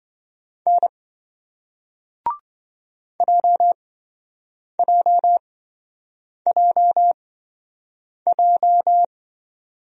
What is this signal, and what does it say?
Morse code, heard as a keyed tone, about 27 wpm, spaced as for 7 wpm, 700 Hz. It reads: N JJJJ